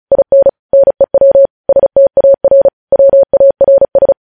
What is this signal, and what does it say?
Morse code, heard as a keyed tone, about 36 wpm, 550 Hz. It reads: IN NEW STAR WARS
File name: podcast / News headlines-1x-35x35